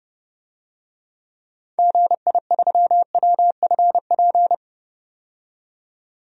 Morse code, heard as a keyed tone, about 30 wpm, 700 Hz. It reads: GI3WFP
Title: DXCC-100 Call Signs - 6 Characters - 30wpm